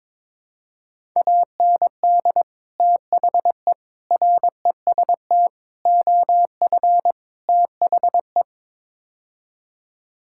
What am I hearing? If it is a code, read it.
Morse code, heard as a keyed tone, about 22 wpm, 700 Hz. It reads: AND THE REST OF THE
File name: practice / 22wpm / Build A Phrase - 5-Word Phrases - 22wpm